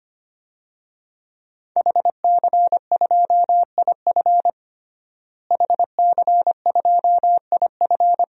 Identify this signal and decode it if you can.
Morse code, heard as a keyed tone, about 25 wpm, 700 Hz. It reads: HC2IF HC2IF
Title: DXCC-100 Call Signs - 5 Characters - Repeated 2x - 25wpm